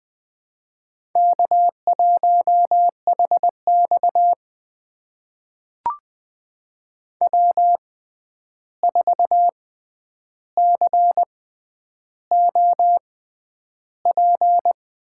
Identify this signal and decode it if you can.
Morse code, heard as a keyed tone, about 20 wpm, spaced as for 10 wpm, 700 Hz. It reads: K1HX W4COP
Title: USA Call Signs - 6x Character Spacing - 20wpm